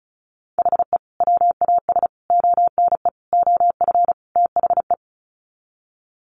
Morse code, heard as a keyed tone, about 34 wpm, 700 Hz. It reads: HE WAS ONE OF THE
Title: Sets of 5 Words 35wpm